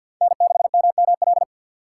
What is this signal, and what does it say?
Morse code, heard as a keyed tone, about 52 wpm, 700 Hz. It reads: N6MML
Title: USA Call Signs - 5 Characters - 3x Character Spacing - 50wpm